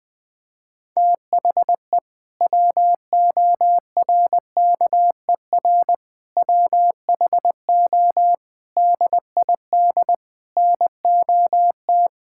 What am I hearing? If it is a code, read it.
Morse code, heard as a keyed tone, about 21 wpm, 700 Hz. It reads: THE WORKER WHO DID NOT
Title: Sentences with 7-Word Phrases - Spoken Beginning - 20wpm